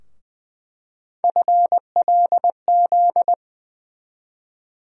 Morse code, heard as a keyed tone, about 21 wpm, 700 Hz. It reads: FLZ
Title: USA Call Sign Postfixes - 3 Letters 20wpm